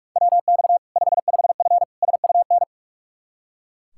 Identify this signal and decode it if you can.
Morse code, heard as a keyed tone, about 42 wpm, 700 Hz. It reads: WX 55F SUN
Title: K1USN SST QSO Practice 45wpm